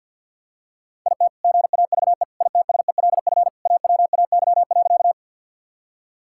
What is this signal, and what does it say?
Morse code, heard as a keyed tone, about 47 wpm, 700 Hz. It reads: IT GAVE ITSELF AWAY.
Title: Sentences from Top 1000 Words - Review Part 5 - 50wpm